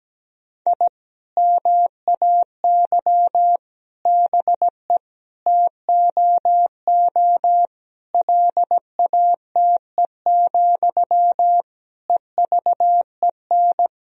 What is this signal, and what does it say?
Morse code, heard as a keyed tone, about 17 wpm, 700 Hz. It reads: I MAY BE TOO LATE, EVEN
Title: Sentences from Top 1000 Words - Review Part 1 - 17wpm